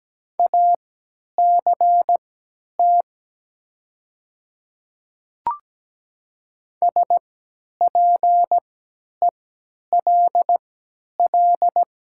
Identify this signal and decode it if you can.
Morse code, heard as a keyed tone, about 17 wpm, spaced as for 10 wpm, 700 Hz. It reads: ACT SPELL
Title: 3x Character Spacing - Top 200 Words - 17wpm